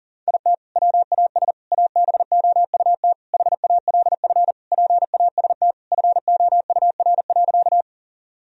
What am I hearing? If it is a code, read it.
Morse code, heard as a keyed tone, about 37 wpm, 700 Hz. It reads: IT WAS ABOUT HALF PAST FOUR.
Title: Sentences from Top 1000 Words - Review Part 3 - 40wpm